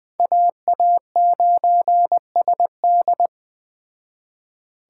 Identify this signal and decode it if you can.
Morse code, heard as a keyed tone, about 20 wpm, 700 Hz. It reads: AA9SD
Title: Speed Racing - USA 2x2 Call Signs 17wpm